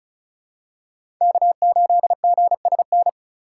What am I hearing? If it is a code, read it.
Morse code, heard as a keyed tone, about 36 wpm, 700 Hz. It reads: K8GSN
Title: USA Call Signs - 4x Character Spacing - 35wpm